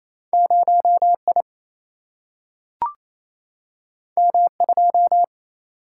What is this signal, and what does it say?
Morse code, heard as a keyed tone, about 28 wpm, 700 Hz. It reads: 0 I M 2
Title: Single-Letter Single-Number 28wpm